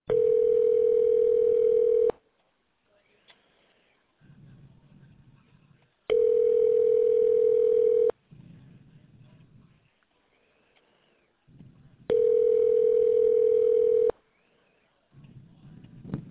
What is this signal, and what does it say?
Ringback tone (North America), 440 + 480 Hz.